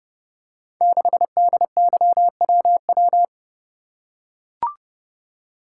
Morse code, heard as a keyed tone, about 28 wpm, 700 Hz. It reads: 6DYWW
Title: Code Groups - 5 Letter:Numbers - 30wpm